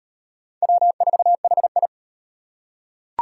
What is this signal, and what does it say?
Morse code, heard as a keyed tone, about 32 wpm, 700 Hz. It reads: W4HI